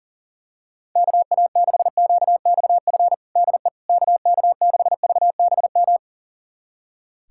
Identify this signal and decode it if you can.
Morse code, heard as a keyed tone, about 40 wpm, 700 Hz. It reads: KA6QXF DE KK6VBM
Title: USA Callsign DE Callsign - 6 Characters - 40wpm